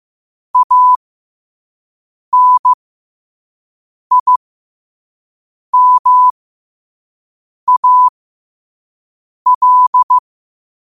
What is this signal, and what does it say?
Morse code, heard as a keyed tone, about 15 wpm, spaced as for 5 wpm, 1000 Hz. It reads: ANIMAL
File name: qotdfiles / 5wpm